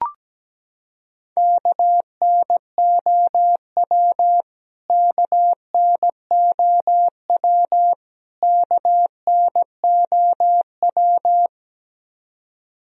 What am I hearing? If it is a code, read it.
Morse code, heard as a keyed tone, about 17 wpm, 700 Hz. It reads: KNOW KNOW KNOW
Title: Top 100 Words Repeated 17wpm